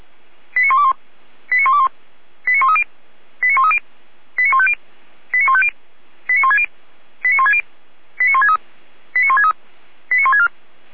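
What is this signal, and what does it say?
Five-tone selective call (ZVEI), heard as quick runs of five tones, tones of about 70 ms. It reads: 89121 89121 89130 89130 89150 89150 89160 89160 89163 89163 89164
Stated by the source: ZVEI five-tone selective call received by SDR, served from audio.f85360.feuerwehr-orlamuende.org (Feuerwehr Orlamünde fire brigade).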